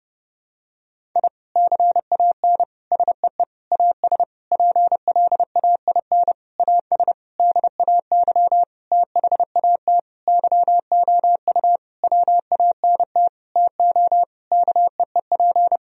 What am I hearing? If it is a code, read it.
Morse code, heard as a keyed tone, about 32 wpm, 700 Hz. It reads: I CAN SEE AS PLAIN AS DAY THAT YOU WANT TO KEEP
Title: Sentences from Top 1000 Words - Review Part 2 - 30wpm